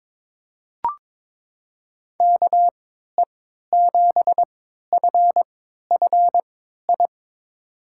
Morse code, heard as a keyed tone, about 22 wpm, spaced as for 13 wpm, 700 Hz. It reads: KE7FFI
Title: USA Call Signs - 3x Character Spacing - 22wpm